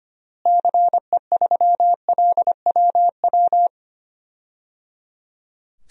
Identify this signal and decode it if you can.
Morse code, heard as a keyed tone, about 25 wpm, 700 Hz. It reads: CE3LWW
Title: DXCC-100 Call Signs - 6 Characters - Repeated 3x - 25wpm